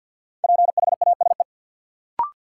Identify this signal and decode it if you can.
Morse code, heard as a keyed tone, about 48 wpm, 700 Hz. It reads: PHASE
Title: Sentences from Top 1200 Words - New Word Drill - 50wpm